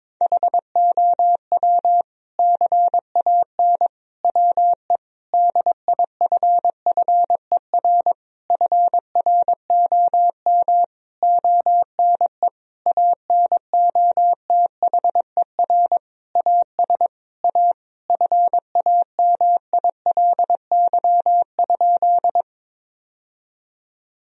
Morse code, heard as a keyed tone, about 21 wpm, 700 Hz. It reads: HOW CAN WE DIFFER FROM ONE ANOTHER AS A FAMILY?
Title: Sentences from Top 1000 Words - Review Part 1 - 22wpm